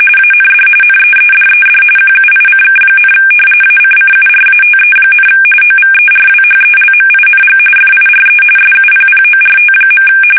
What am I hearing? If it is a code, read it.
RTTY, about 45.45 baud. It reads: AND NEEDED QUICK, ACCURATE, OVERWHELMING FIRE SUPPORT.
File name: fsk850 / battleshipnj